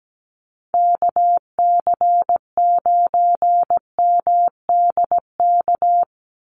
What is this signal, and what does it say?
Morse code, heard as a keyed tone, about 17 wpm, 700 Hz. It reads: KC9MDK
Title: USA Call Signs 17wpm